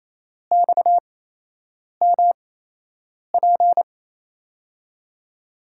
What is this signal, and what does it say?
Morse code, heard as a keyed tone, about 27 wpm, spaced as for 8 wpm, 700 Hz. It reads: XMP
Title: USA Call Signs - 5 Characters - 8x Character Spacing - 28wpm